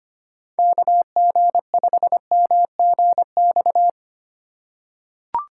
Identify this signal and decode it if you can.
Morse code, heard as a keyed tone, about 25 wpm, 700 Hz. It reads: KG5MGX
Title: USA Call Signs - 3x Character Spacing - 25wpm